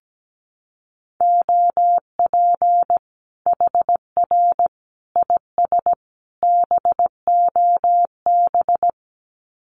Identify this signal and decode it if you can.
Morse code, heard as a keyed tone, about 17 wpm, 700 Hz. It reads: OP HR IS BOB
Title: Common QSO Phrases 17wpm